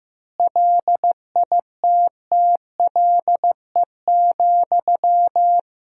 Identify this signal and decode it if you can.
Morse code, heard as a keyed tone, about 16 wpm, 700 Hz. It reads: LITTLE,
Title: 4x Word-Spacing - Sentences from Top 400 Words - 15wpm